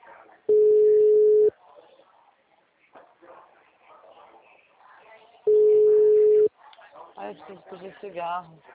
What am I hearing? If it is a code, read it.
Ringback tone (Europe, 425 Hz), with speech.